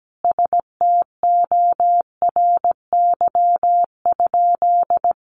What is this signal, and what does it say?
Morse code, heard as a keyed tone, about 17 wpm, 700 Hz. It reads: STORY?
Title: Sentences from Top 400 Words 17wpm